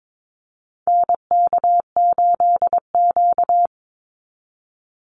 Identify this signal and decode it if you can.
Morse code, heard as a keyed tone, about 22 wpm, 700 Hz. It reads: NK8Q
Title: NA SOTA Activator Callsigns 22wpm